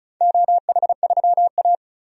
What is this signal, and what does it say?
Morse code, heard as a keyed tone, about 36 wpm, 690 Hz. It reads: OH3A
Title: DXCC-100 Call Signs - 4 Characters - 35wpm